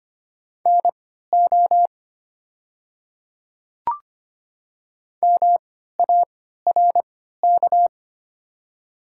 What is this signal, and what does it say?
Morse code, heard as a keyed tone, about 25 wpm, spaced as for 14 wpm, 700 Hz. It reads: NO MARK
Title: 3x Character Spacing - Top 300 Words - 25wpm